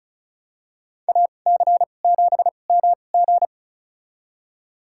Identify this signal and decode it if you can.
Morse code, heard as a keyed tone, about 36 wpm, spaced as for 25 wpm, 700 Hz. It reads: AC7MG
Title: USA Call Signs - 2x Character Spacing - 35wpm